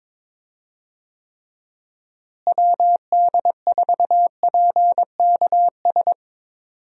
Morse code, heard as a keyed tone, about 22 wpm, 700 Hz. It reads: WD4PKS
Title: USA Call Signs - 6 Characters - Repeated 3x - 22wpm